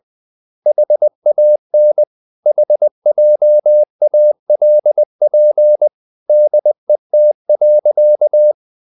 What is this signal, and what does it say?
Morse code, heard as a keyed tone, about 20 wpm, 600 Hz. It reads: HAN HJALP DET.
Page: Ferie 20 wpm